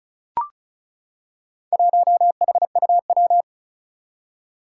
Morse code, heard as a keyed tone, about 36 wpm, 700 Hz. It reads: 1HUW
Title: Four Challenging Letter-Number Combinations 35wpm